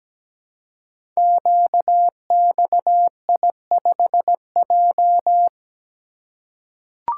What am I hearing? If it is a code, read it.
Morse code, heard as a keyed tone, about 17 wpm, 700 Hz. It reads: QXI5J
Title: Code Groups - 5 Letter:Numbers - 17wpm